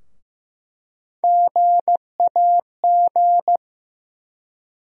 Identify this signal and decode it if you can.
Morse code, heard as a keyed tone, about 15 wpm, 700 Hz. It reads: GAG
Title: USA Call Sign Postfixes - 3 Letters 15wpm